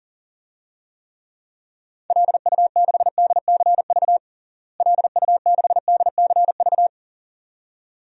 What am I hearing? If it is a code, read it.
Morse code, heard as a keyed tone, about 37 wpm, 700 Hz. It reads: LU6DCV LU6DCV
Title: DXCC-100 Call Signs - 6 Characters - Repeated 2x - 40wpm